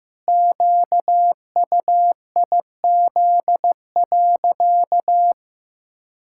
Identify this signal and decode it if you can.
Morse code, heard as a keyed tone, about 15 wpm, 700 Hz. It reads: QUIZ.
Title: Pangram Challenge v2 - 15wpm